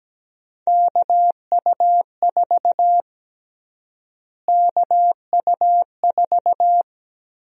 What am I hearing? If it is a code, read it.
Morse code, heard as a keyed tone, about 17 wpm, 700 Hz. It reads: KU4 KU4
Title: USA Call Sign Prefixes - Repeated 2x - 17wpm